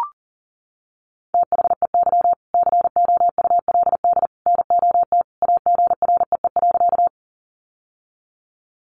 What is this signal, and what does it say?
Morse code, heard as a keyed tone, about 40 wpm, 700 Hz. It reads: THEY COULD NOT AGREE.
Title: Sentences from Top 1100 Words 40wpm